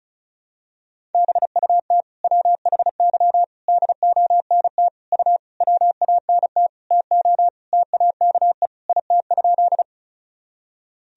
Morse code, heard as a keyed tone, about 36 wpm, 700 Hz. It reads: BUT WHY DONT U WANT TO TAKE IT?
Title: Sentences from Top 300 Words and Abbreviations 35wpm